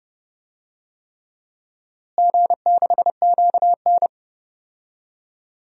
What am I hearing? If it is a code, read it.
Morse code, heard as a keyed tone, about 30 wpm, 700 Hz. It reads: G6QN
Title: DXCC-100 Call Signs - 4 Characters - Repeated 3x - 30wpm